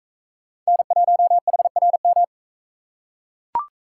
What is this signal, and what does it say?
Morse code, heard as a keyed tone, about 42 wpm, 700 Hz. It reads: N1HRM